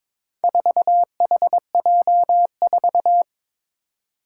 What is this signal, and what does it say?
Morse code, heard as a keyed tone, about 23 wpm, 700 Hz. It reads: 4HJ4
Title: Four Challenging Letter-Number Combinations 22wpm